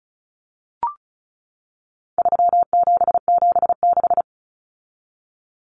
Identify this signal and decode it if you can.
Morse code, heard as a keyed tone, about 34 wpm, 700 Hz. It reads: 3776
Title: Numbers - 4-Digit 35wpm